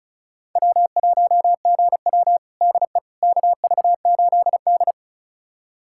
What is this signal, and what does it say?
Morse code, heard as a keyed tone, about 36 wpm, 700 Hz. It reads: W1GW DE KV8D
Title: USA Callsign DE Callsign - 4 Characters - 35wpm